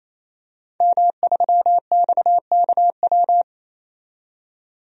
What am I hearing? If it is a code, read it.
Morse code, heard as a keyed tone, about 28 wpm, 700 Hz. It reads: M3XKW
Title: DXCC-100 Call Signs - 5 Characters - 28wpm